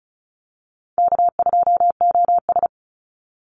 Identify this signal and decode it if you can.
Morse code, heard as a keyed tone, about 34 wpm, 700 Hz. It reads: K2OS
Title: Call Signs Repeated 3x - 35wpm